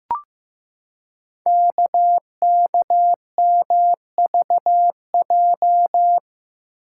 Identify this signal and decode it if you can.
Morse code, heard as a keyed tone, about 15 wpm, 700 Hz. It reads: KKMVJ
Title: Code Groups - 5 Letters - 15wpm